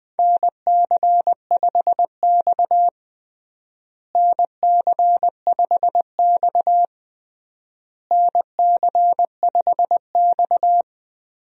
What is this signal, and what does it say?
Morse code, heard as a keyed tone, about 20 wpm, 700 Hz. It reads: NC5X NC5X NC5X
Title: USA Call Signs - 4 Characters - Repeated 3x - 20wpm